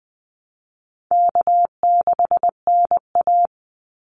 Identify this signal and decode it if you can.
Morse code, heard as a keyed tone, about 20 wpm, 700 Hz. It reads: K6NA
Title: Easy Call Signs Repeated 3x - 20wpm